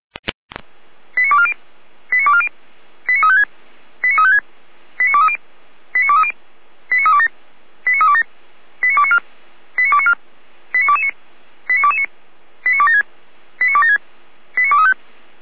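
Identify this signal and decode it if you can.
Five-tone selective call (ZVEI), heard as quick runs of five tones, tones of about 70 ms. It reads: 89240 89240 89356 89356 89230 89230 89327 89327 89294 89294 89229 89229 89276 89276 89235
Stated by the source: ZVEI five-tone call 89164 received by SDR, from the Feuerwehr Orlamünde audio server.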